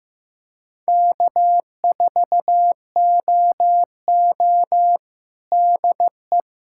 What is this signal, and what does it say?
Morse code, heard as a keyed tone, about 15 wpm, 700 Hz. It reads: K4OO DE
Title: USA Callsign DE Callsign - 4 Characters - 15wpm